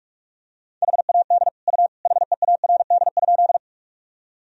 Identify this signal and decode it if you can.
Morse code, heard as a keyed tone, about 46 wpm, 700 Hz. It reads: HAD U HEARD?
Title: Sentences from Top 400 Words and Abbreviations 45wpm